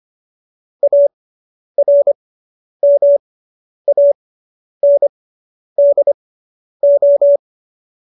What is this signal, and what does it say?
Morse code, heard as a keyed tone, about 25 wpm, spaced as for 10 wpm, 570 Hz. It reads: ARMANDO